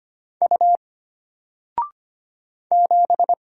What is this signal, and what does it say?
Morse code, heard as a keyed tone, about 25 wpm, 700 Hz. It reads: U 7
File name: Single-Letter-Number-25wpm